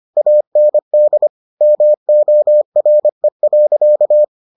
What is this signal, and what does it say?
Morse code, heard as a keyed tone, about 25 wpm, 600 Hz. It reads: AND MORE.